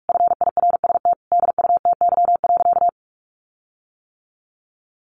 Morse code, heard as a keyed tone, about 44 wpm, 700 Hz. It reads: FIRST DUTY.
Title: Sentences from Top 1500 Words 45wpm